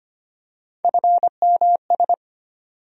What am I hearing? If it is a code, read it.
Morse code, heard as a keyed tone, about 25 wpm, 700 Hz. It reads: FMS
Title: USA Call Sign Postfixes - 3 Letters 25wpm